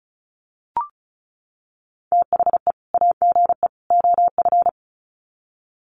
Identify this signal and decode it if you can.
Morse code, heard as a keyed tone, about 34 wpm, 700 Hz. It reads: THE AGE OF
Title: Sets of 3 Words 35wpm